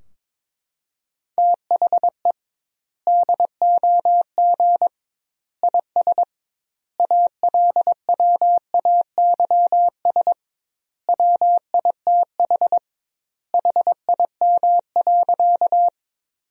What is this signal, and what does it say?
Morse code, heard as a keyed tone, about 22 wpm, 700 Hz. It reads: THE DOG IS ALWAYS WITH HIM.
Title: 2x Word-Spacing - Sentences from Top 400 Words - 22wpm